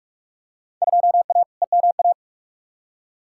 Morse code, heard as a keyed tone, about 46 wpm, 700 Hz. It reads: 2A EMA
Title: Field Day Exchanges Encore - 45wpm